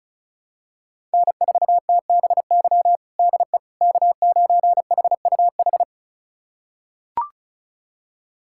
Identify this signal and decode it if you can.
Morse code, heard as a keyed tone, about 36 wpm, 700 Hz. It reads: N4TBY DE K9HUH
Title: USA Callsign DE Callsign - 5 Characters - 35wpm